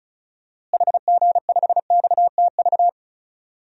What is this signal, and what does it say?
Morse code, heard as a keyed tone, about 36 wpm, 700 Hz. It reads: HG5XTV
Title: DXCC-100 Call Signs - 6 Characters - Repeated 2x - 35wpm